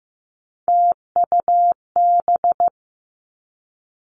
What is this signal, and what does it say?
Morse code, heard as a keyed tone, about 15 wpm, 700 Hz. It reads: TUB
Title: Three Challenging Letter-Number Combinations 15wpm